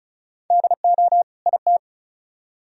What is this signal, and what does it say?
Morse code, heard as a keyed tone, about 49 wpm, spaced as for 42 wpm, 700 Hz. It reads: DO IT
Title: Speed-Racing - Sets of 2 Words 35wpm